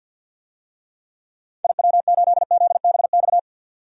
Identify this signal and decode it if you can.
Morse code, heard as a keyed tone, about 47 wpm, 700 Hz. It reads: IW8ZBX
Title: DXCC-100 Call Signs - 6 Characters - 50wpm